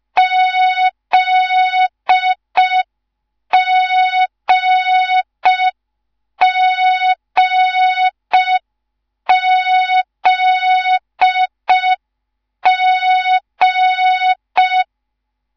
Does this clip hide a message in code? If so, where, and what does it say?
Yes: Morse code from 0.2 to 14.8 s, heard as a keyed tone, about 5 wpm, 3700 Hz. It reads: ZGGZG